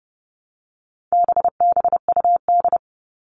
Morse code, heard as a keyed tone, about 30 wpm, 700 Hz. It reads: BBUD
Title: All Cut Numbers - 4-Digit 30wpm